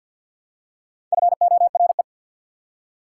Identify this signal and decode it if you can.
Morse code, heard as a keyed tone, about 53 wpm, 690 Hz. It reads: FORE
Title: Prefixes and Suffixes 50wpm